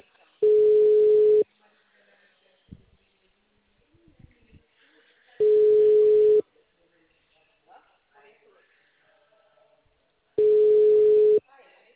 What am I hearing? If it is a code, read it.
Ringback tone (Europe, 425 Hz), with speech.
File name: out- -2007- - - .1039